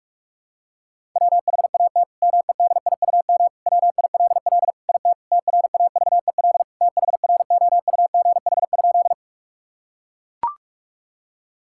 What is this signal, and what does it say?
Morse code, heard as a keyed tone, about 41 wpm, 700 Hz. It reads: WHAT MEDIUM WILL IT TRAVEL THROUGH?
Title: Sentences from Top 1000 Words - Review Part 6 - 45wpm